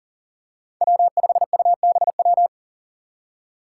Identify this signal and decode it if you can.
Morse code, heard as a keyed tone, about 39 wpm, 700 Hz. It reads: W5UBW
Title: Five Challenging Letter-Number Combinations 40wpm